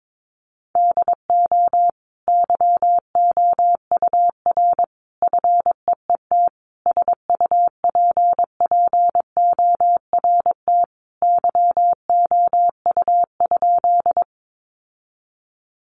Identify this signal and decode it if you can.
Morse code, heard as a keyed tone, about 22 wpm, 700 Hz. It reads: DO YOUR FEET SUPPORT YOU?
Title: Sentences from Top 800 Words 22wpm